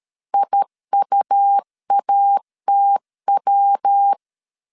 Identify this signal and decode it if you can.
Morse code, heard as a keyed tone, about 13 wpm, 800 Hz. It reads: IUATW
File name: CW / DET / 1 3 1500C S A 12wpm 2905